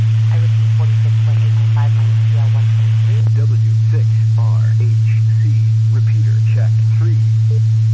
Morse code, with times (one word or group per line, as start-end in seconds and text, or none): none